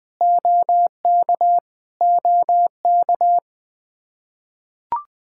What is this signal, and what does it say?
Morse code, heard as a keyed tone, about 20 wpm, 700 Hz. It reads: OK OK
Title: Practice QSO - Exchange 1 - Beginning - 20wpm